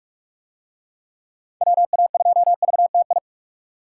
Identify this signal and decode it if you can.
Morse code, heard as a keyed tone, about 44 wpm, 700 Hz. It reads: WA2VTI